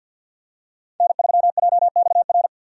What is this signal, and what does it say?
Morse code, heard as a keyed tone, about 53 wpm, 700 Hz. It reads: N3JXR
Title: USA Call Signs - 5 Characters - 50wpm